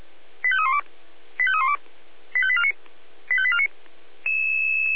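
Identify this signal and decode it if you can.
Five-tone selective call (ZVEI), heard as quick runs of five tones, tones of about 80 ms. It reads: 85312 85312 85849 85849
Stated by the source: Five-tone ZVEI call 85310, received by SDR; fire brigade Feuerwehr Orlamünde.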